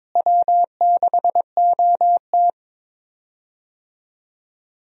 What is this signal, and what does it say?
Morse code, heard as a keyed tone, about 22 wpm, 700 Hz. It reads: W6OT
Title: USA Call Signs - 4 Characters - Repeated 3x - 22wpm